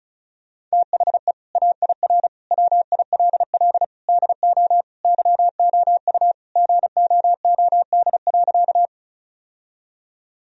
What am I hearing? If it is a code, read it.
Morse code, heard as a keyed tone, about 34 wpm, 700 Hz. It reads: THE AIR WILL DO YOU GOOD.
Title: Sentences from Top 1000 Words - Review Part 1 - 35wpm